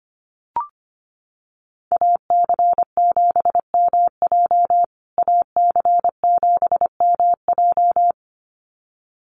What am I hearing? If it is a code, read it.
Morse code, heard as a keyed tone, about 25 wpm, 700 Hz. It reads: AC7MJ AC7MJ
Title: USA Call Signs Repeated 2x 25wpm